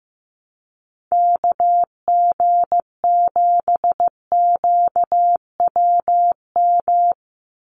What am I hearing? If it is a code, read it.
Morse code, heard as a keyed tone, about 15 wpm, 700 Hz. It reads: KG7QWM
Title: USA Call Signs 15wpm